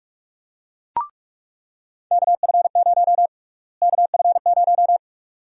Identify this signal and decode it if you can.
Morse code, heard as a keyed tone, about 42 wpm, 700 Hz. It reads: KF0 KF0
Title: USA Call Sign Prefixes - Repeated 2x - 45wpm